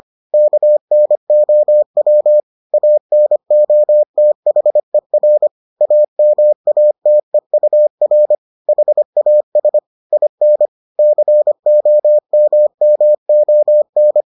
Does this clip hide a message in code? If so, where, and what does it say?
Yes: Morse code from 0.3 to 14.2 s, heard as a keyed tone, about 25 wpm, 600 Hz. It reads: KNOW ANOTHER AMATEUR HAS IN COMMON